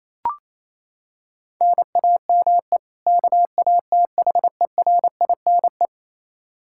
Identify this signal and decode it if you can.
Morse code, heard as a keyed tone, about 28 wpm, 700 Hz. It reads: NAME KATHERINE
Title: Practice QSO - Exchange 1 - Beginning - 28wpm